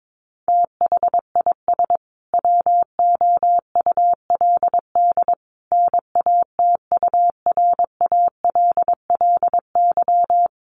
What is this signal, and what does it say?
Morse code, heard as a keyed tone, about 22 wpm, 700 Hz. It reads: THIS WOULD NATURALLY
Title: Sentences from Top 1300 Words 22wpm